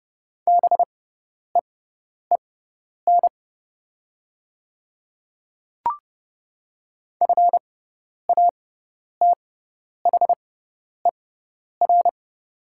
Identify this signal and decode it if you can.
Morse code, heard as a keyed tone, about 30 wpm, spaced as for 10 wpm, 700 Hz. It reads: BEEN FATHER
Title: 6x Character Spacing - Top 200 Words - 30wpm